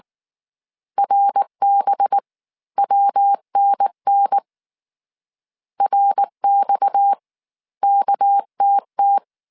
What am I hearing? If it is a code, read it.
Morse code, heard as a keyed tone, about 19 wpm, 800 Hz. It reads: RB WNN RX KTT
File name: CW / DET / 1 6 1500C S C 18wpm 2356